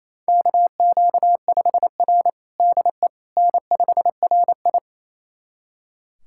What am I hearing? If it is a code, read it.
Morse code, heard as a keyed tone, about 28 wpm, 700 Hz. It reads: KQ5R DE N5RI
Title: USA Callsign DE Callsign - 4 Characters - 28wpm